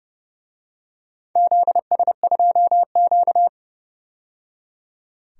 Morse code, heard as a keyed tone, about 28 wpm, 700 Hz. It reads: ZS2Q
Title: DXCC-100 Call Signs - 4 Characters - 30wpm